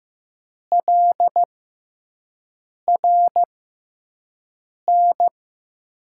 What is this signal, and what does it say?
Morse code, heard as a keyed tone, about 15 wpm, spaced as for 5 wpm, 700 Hz. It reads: LRN
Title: USA Call Signs - 5 Characters - 6x Character Spacing - 15wpm